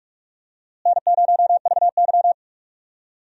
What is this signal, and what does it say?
Morse code, heard as a keyed tone, about 46 wpm, 700 Hz. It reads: N0VY
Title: USA Call Signs - 4 Characters - Repeated 2x - 45wpm